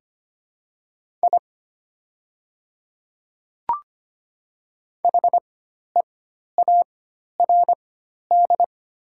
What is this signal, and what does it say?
Morse code, heard as a keyed tone, about 25 wpm, spaced as for 12 wpm, 700 Hz. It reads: I HEARD
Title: 4x Character Spacing - Top 500 Words - 25wpm